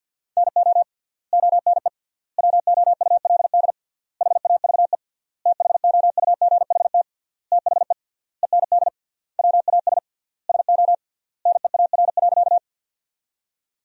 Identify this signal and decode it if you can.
Morse code, heard as a keyed tone, about 50 wpm, 700 Hz. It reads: NO ONE WOULD HAVE THOUGHT THE END WAS SO NEAR.
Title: 3x Word-Spacing - Sentences from Top 300 Words - 50wpm